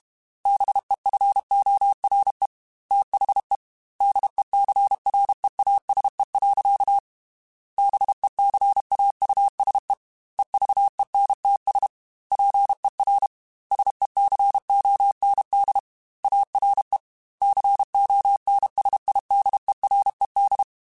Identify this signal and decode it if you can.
Morse code, heard as a keyed tone, about 32 wpm, 800 Hz. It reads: BEFORE THE DECREASE. BECAUSE EVENTS PER SECOND ARE CONSIDERED
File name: Technical Subjects / Technical Subjects-mp3s / XC @30